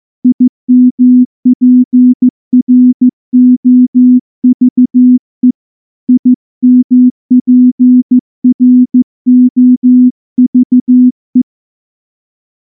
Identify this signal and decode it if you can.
Morse code, heard as a keyed tone, about 14 wpm, 250 Hz. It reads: IMPROVE IMPROVE
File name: morse-training-15WPM